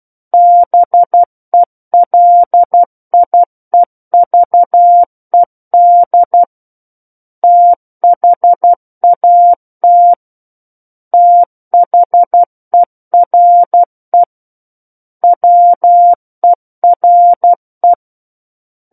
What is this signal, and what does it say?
Morse code, heard as a keyed tone, about 12 wpm, 700 Hz. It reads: BELIEVED THAT THERE WERE
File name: Book One / 03-Chapter 3 - 12 WPM